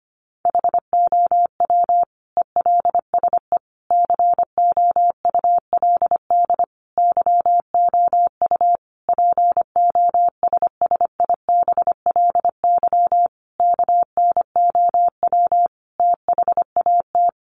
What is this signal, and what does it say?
Morse code, heard as a keyed tone, about 25 wpm, 700 Hz. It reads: HOW ELSE COULD YOU POSSIBLY KNOW THAT
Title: Sentences from Top 1300 Words 25wpm